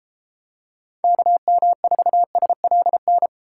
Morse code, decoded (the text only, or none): KM4SLN